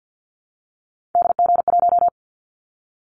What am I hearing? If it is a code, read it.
Morse code, heard as a keyed tone, about 50 wpm, 700 Hz. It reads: DG1
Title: Callsign Prefixes for Top 25 DXCC - 50wpm